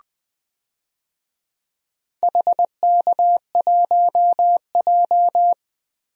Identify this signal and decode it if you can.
Morse code, heard as a keyed tone, about 21 wpm, 700 Hz. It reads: HK1J
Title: DXCC-100 Call Signs - 4 Characters - Repeated 2x - 20wpm